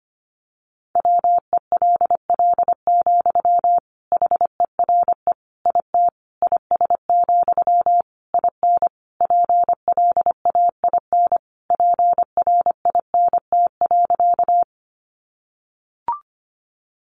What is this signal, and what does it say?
Morse code, heard as a keyed tone, about 25 wpm, 700 Hz. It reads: WELL, HERE IT IS, IN PLAIN PRINT.
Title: Sentences from Top 1400 Words 25wpm